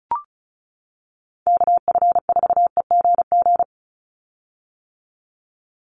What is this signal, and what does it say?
Morse code, heard as a keyed tone, about 34 wpm, 700 Hz. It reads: KF4EGG
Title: USA Call Signs 35wpm